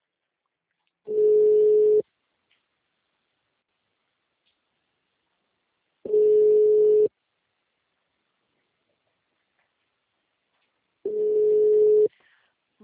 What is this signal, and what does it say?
Ringback tone (Europe, 425 Hz).